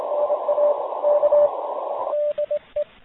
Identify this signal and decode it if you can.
Morse code, heard as a keyed tone, about 24 wpm, 590 Hz, with radio noise. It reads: NA K DE
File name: fp-n6ra-cwqso